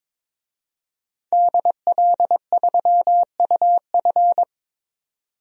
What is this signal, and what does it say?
Morse code, heard as a keyed tone, about 22 wpm, 700 Hz. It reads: DL3UF
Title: DXCC-100 Call Signs - 5 Characters - 22wpm